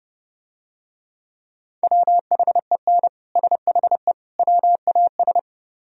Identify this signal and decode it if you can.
Morse code, heard as a keyed tone, about 32 wpm, 700 Hz. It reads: WHEN SHE WAS